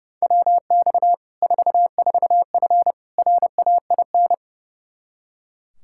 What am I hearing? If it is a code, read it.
Morse code, heard as a keyed tone, about 30 wpm, 700 Hz. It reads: WX 44F RAIN
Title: Practice QSO - Exchange 2 - Info - 30wpm